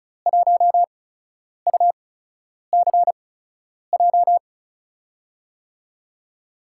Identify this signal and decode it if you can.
Morse code, heard as a keyed tone, about 36 wpm, spaced as for 10 wpm, 700 Hz. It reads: 1UCJ
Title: USA Call Signs - 5 Characters - 8x Character Spacing - 35wpm